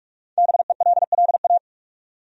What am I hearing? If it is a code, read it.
Morse code, heard as a keyed tone, about 47 wpm, 700 Hz. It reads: BELLA
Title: USA Common Names - Max 5 Letters - 45wpm